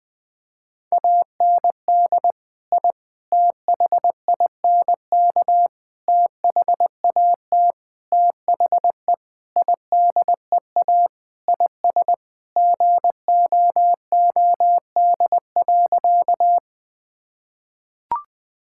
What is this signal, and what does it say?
Morse code, heard as a keyed tone, about 20 wpm, 700 Hz. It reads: AND I THINK THAT THE IDEA IS GOOD.
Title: Sentences with 5-Word Phrases - Spoken Beginning - 20wpm